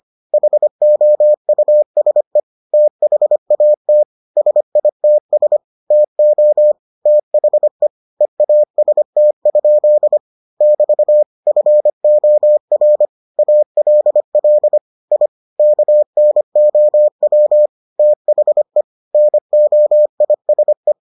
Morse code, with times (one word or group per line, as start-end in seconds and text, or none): 0.3-2.4 HOUSE
2.7-4.0 THAT
4.4-5.6 SITS
5.9-6.7 TO
7.1-7.9 THE
8.2-10.2 EAST?
10.6-11.2 <BT>
11.5-13.1 FOR
13.4-14.8 ALL
15.1-15.3 I
15.6-17.7 KNOW
18.0-18.8 THE
19.1-20.9 NOISE